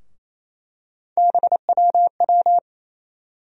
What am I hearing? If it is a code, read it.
Morse code, heard as a keyed tone, about 28 wpm, 700 Hz. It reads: BWW
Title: USA Call Sign Postfixes - 3 Letters 28wpm